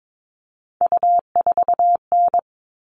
Morse code, heard as a keyed tone, about 22 wpm, 700 Hz. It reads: U4N
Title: All Cut Numbers - 3-Digit 22wpm